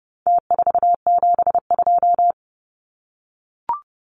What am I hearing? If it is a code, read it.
Morse code, heard as a keyed tone, about 30 wpm, 700 Hz. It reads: T472
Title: Numbers - Contest Style - 4-Digit 30wpm